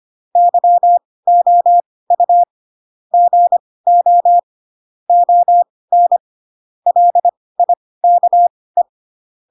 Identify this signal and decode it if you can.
Morse code, heard as a keyed tone, about 25 wpm, spaced as for 18 wpm, 700 Hz. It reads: YOU GO ON LIKE
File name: Pinocchio / Pinocchio ch4 12 25 ik1hge